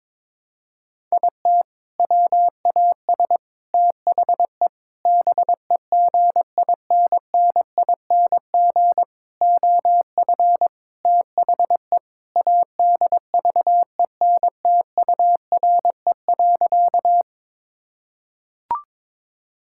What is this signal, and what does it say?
Morse code, heard as a keyed tone, about 22 wpm, 700 Hz. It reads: IT WAS THE BEGINNING OF THE ADVENTURE.
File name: practice / 22wpm / Sentences with 4-Word Phrases - 22wpm